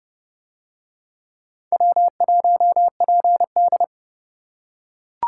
Morse code, heard as a keyed tone, about 29 wpm, 700 Hz. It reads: W1PD